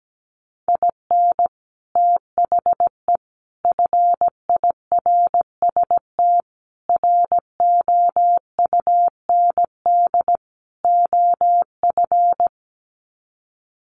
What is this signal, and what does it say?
Morse code, heard as a keyed tone, about 17 wpm, 700 Hz. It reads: IN THE FIRST ROUND OF
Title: Sets of 5 Words Encore 17wpm